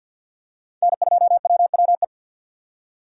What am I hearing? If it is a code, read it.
Morse code, heard as a keyed tone, about 52 wpm, 700 Hz. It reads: N2WWE
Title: USA Call Signs - 2x Character Spacing - 50wpm